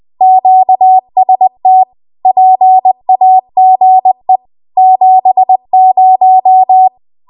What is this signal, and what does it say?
Morse code, heard as a keyed tone, about 20 wpm, 750 Hz. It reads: QST PAGE 70
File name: Archive / 20 WPM / 20WPM